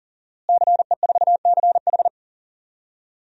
Morse code, heard as a keyed tone, about 40 wpm, 700 Hz. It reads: CE4CH
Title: DXCC-100 Call Signs - 5 Characters - Repeated 2x - 40wpm